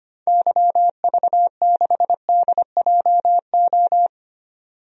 Morse code, heard as a keyed tone, about 25 wpm, 700 Hz. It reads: YV6DJO
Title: DXCC-100 Call Signs - 6 Characters - 25wpm